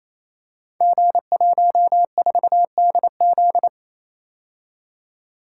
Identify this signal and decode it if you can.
Morse code, heard as a keyed tone, about 28 wpm, 700 Hz. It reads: G14DZ
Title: Code Groups - 5 Letter:Numbers - 28wpm